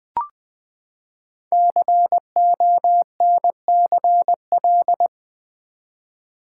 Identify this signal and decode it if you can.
Morse code, heard as a keyed tone, about 20 wpm, 700 Hz. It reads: CONCL
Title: Build A Word - 5 Letters At A Time - 10-Letter Words - 20wpm